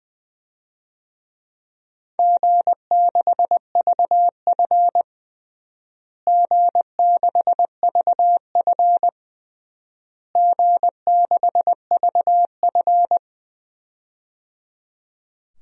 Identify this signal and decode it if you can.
Morse code, heard as a keyed tone, about 21 wpm, 700 Hz. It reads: G6VF G6VF G6VF
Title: DXCC-100 Call Signs - 4 Characters - Repeated 3x - 20wpm